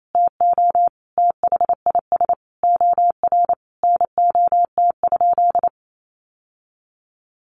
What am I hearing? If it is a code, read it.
Morse code, heard as a keyed tone, about 28 wpm, 700 Hz. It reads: TO THIS OR NOT?
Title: Sentences from Top 1100 Words 28wpm